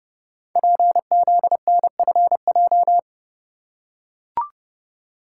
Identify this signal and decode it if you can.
Morse code, heard as a keyed tone, about 29 wpm, 700 Hz. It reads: PZNFJ
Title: Code Groups - 5 Letters - 30wpm